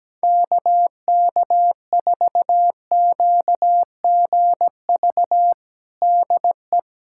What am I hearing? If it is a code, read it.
Morse code, heard as a keyed tone, about 17 wpm, 700 Hz. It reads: KK4QGV DE
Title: USA Callsign DE Callsign 17wpm